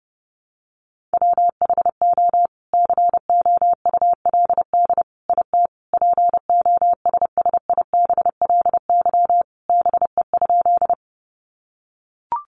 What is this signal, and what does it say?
Morse code, heard as a keyed tone, about 30 wpm, 700 Hz. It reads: WHO COULD IT POSSIBLY BE?
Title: Sentences from Top 1300 Words 30wpm